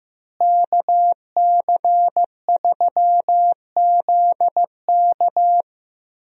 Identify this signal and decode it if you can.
Morse code, heard as a keyed tone, about 15 wpm, 700 Hz. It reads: KC3ZK
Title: USA Call Signs - 5 Characters - Repeated 2x - 15wpm